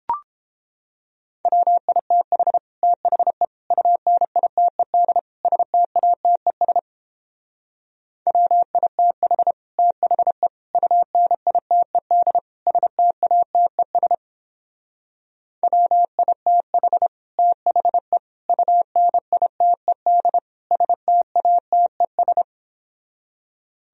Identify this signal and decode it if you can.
Morse code, heard as a keyed tone, about 29 wpm, 700 Hz. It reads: WITH THE UNITED STATES WITH THE UNITED STATES WITH THE UNITED STATES